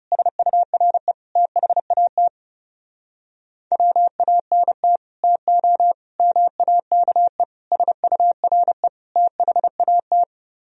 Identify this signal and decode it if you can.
Morse code, heard as a keyed tone, about 40 wpm, 700 Hz. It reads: SURE THAT WANT TO MAKE SURE THAT